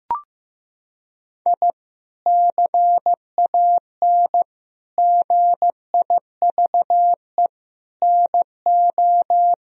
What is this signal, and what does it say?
Morse code, heard as a keyed tone, about 15 wpm, 700 Hz. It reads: I CAN GIVE NO
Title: Sentences from Top 1000 Words - Review Part 3 - 15wpm